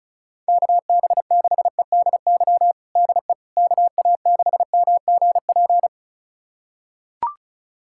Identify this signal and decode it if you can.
Morse code, heard as a keyed tone, about 36 wpm, 700 Hz. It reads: KB6EDY DE KA6MGP
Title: USA Callsign DE Callsign - 6 Characters - 35wpm